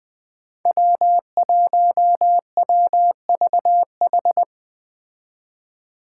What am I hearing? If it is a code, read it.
Morse code, heard as a keyed tone, about 20 wpm, 700 Hz. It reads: W1WVH
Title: Five Challenging Letter-Number Combinations 20wpm